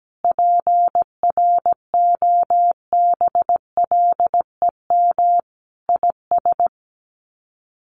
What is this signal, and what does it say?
Morse code, heard as a keyed tone, about 17 wpm, 700 Hz. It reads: PROBLEM IS
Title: Sets of 5 Words Encore2 17wpm